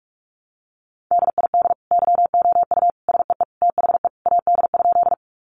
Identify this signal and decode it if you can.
Morse code, heard as a keyed tone, about 44 wpm, 700 Hz. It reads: DID YOU SEE THE AD?
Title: Sentences from Top 1300 Words 45wpm